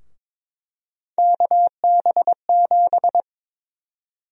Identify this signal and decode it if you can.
Morse code, heard as a keyed tone, about 22 wpm, 700 Hz. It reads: KB7
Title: USA Call Sign Prefixes - 22wpm